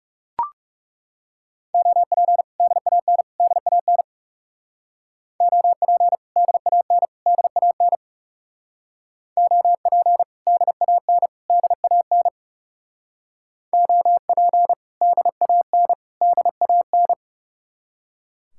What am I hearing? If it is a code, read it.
Morse code, heard as a keyed tone, about 35 wpm, 700 Hz. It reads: OP DAN DAN OP DAN DAN OP DAN DAN OP DAN DAN
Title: Speed-Racing Common QSO Phrases - 30wpm